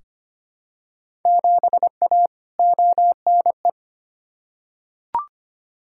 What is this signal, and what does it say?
Morse code, heard as a keyed tone, about 25 wpm, 700 Hz. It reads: 7A ONE
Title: Field Day Exchanges Encore - 25wpm